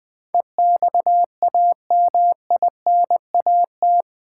Morse code, heard as a keyed tone, about 20 wpm, 700 Hz. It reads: EXAMINAT
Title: Build A Word - 4 Letters At A Time - 11-Letter Words - 20wpm